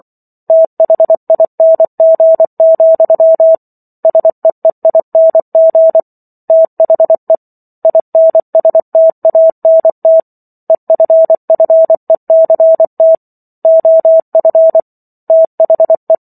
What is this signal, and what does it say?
Morse code, heard as a keyed tone, about 24 wpm, 650 Hz. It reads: THING, SEEING THE INSTANT EFFECT OF THE